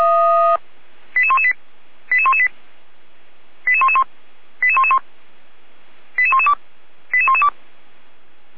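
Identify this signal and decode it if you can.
Five-tone selective call (ZVEI), heard as quick runs of five tones, tones of about 70 ms. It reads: 88118 88118 88111 88111 88112 88112
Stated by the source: ZVEI five-tone call received on SDR, fire brigade Orlamünde (feuerwehr-orlamuende) recording.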